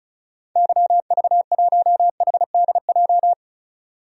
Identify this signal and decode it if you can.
Morse code, heard as a keyed tone, about 36 wpm, 700 Hz. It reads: YV1HDJ
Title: DXCC-100 Call Signs - 6 Characters - 35wpm